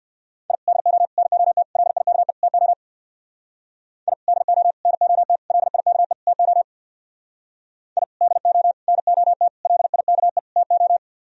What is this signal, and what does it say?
Morse code, heard as a keyed tone, about 59 wpm, 700 Hz. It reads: I DO NOT LIKE TO I DO NOT LIKE TO I DO NOT LIKE TO
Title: Speed-Racing - Sets of 5 Words 50wpm